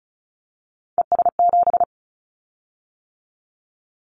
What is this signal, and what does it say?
Morse code, heard as a keyed tone, about 34 wpm, 700 Hz. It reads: ES7
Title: Callsign Prefixes for Top 100 DXCC - 35wpm